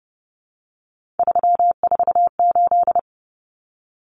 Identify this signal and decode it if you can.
Morse code, heard as a keyed tone, about 30 wpm, 700 Hz. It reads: 348
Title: Common Cut Numbers - 3-Digit 30wpm